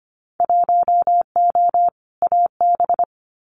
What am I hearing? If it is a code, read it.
Morse code, heard as a keyed tone, about 25 wpm, 700 Hz. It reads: 1O AB
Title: Winter Field Day Exchanges Encore 25wpm